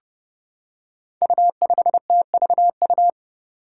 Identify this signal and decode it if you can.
Morse code, heard as a keyed tone, about 30 wpm, 700 Hz. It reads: U5TVU
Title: Five Challenging Letter-Number Combinations 30wpm